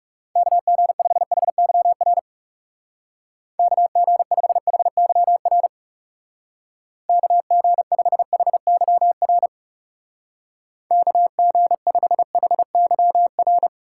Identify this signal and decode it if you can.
Morse code, heard as a keyed tone, about 36 wpm, 700 Hz. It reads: KG5HYR KG5HYR KG5HYR KG5HYR
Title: Speed Racing - USA 2x3 Call Signs 30wpm